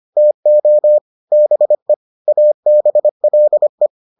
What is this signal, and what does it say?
Morse code, heard as a keyed tone, about 25 wpm, 600 Hz. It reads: TO BE ABLE